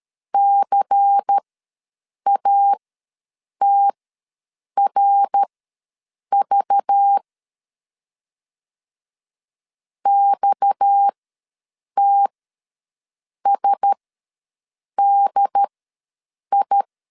Morse code, heard as a keyed tone, about 13 wpm, spaced as for 7 wpm, 800 Hz. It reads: CATRV XTSDI
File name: LEZIONI / 6 PKX0 / 6 PKX ws 12wpm 3106